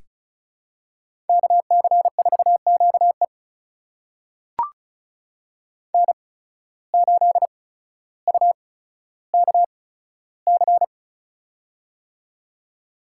Morse code, heard as a keyed tone, about 36 wpm, spaced as for 15 wpm, 700 Hz. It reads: KC4QE N8UKC
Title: USA Call Signs - 5 Characters - 8x Character Spacing - 35wpm